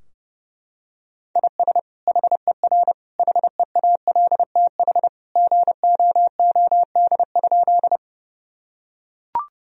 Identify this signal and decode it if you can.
Morse code, heard as a keyed tone, about 32 wpm, 700 Hz. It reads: IS HER HEALTH GOOD?
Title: Sentences from Top 1000 Words - Review Part 3 - 30wpm